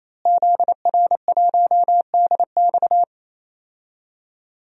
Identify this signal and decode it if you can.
Morse code, heard as a keyed tone, about 28 wpm, 700 Hz. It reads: ZR1DX
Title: DXCC-100 Call Signs - 5 Characters - Repeated 3x - 28wpm